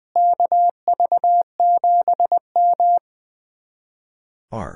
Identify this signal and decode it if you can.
Morse code, heard as a keyed tone, about 20 wpm, 700 Hz. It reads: KV7M
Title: Speed-Racing Common QSO Phrases Encore - 20wpm